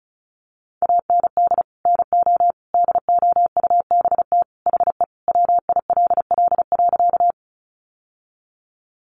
Morse code, heard as a keyed tone, about 34 wpm, 700 Hz. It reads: AND NO DOUBT HE WILL.
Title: Sentences from Top 1500 Words 35wpm